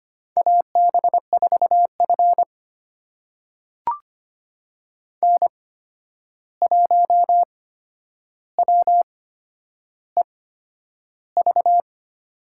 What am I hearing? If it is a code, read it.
Morse code, heard as a keyed tone, about 25 wpm, spaced as for 10 wpm, 700 Hz. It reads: AB4F N1WEV
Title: USA Call Signs - 8x Character Spacing - 25wpm